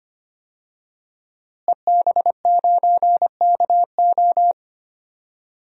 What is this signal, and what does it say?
Morse code, heard as a keyed tone, about 25 wpm, 700 Hz. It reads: EB9KO
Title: DXCC-100 Call Signs - 5 Characters - Repeated 3x - 25wpm